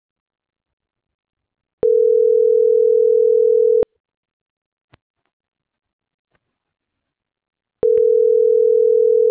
Ringback tone (North America), 440 + 480 Hz.